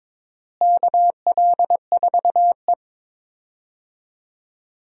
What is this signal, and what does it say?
Morse code, heard as a keyed tone, about 22 wpm, 700 Hz. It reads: KL4E
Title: USA Call Signs - 4 Characters - Repeated 2x - 22wpm